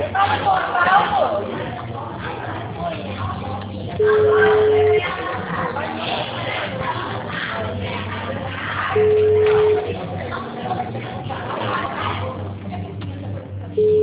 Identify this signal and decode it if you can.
Ringback tone (Europe, 425 Hz), with speech.